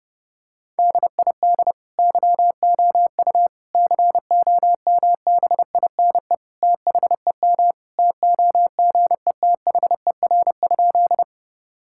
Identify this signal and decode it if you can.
Morse code, heard as a keyed tone, about 30 wpm, 700 Hz. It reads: DID YOU COMBINE THEM TOGETHER?
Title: Sentences from Top 1600 Words 30wpm